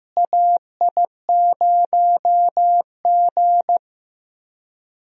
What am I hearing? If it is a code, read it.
Morse code, heard as a keyed tone, about 15 wpm, 700 Hz. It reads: AI0G
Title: USA Call Signs - 4 Characters - Repeated 3x - 15wpm